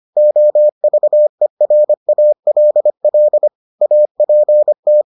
Morse code, heard as a keyed tone, about 25 wpm, 600 Hz. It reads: OVERALL APT